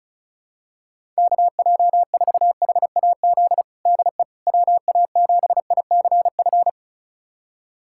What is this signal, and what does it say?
Morse code, heard as a keyed tone, about 36 wpm, 700 Hz. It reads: KJ4HAZ DE WA7ICF
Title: USA Callsign DE Callsign - 6 Characters - 35wpm